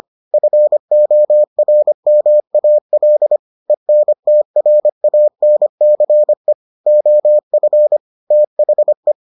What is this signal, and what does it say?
Morse code, heard as a keyed tone, about 25 wpm, 600 Hz. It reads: FORMAL ENTRANCE OF THE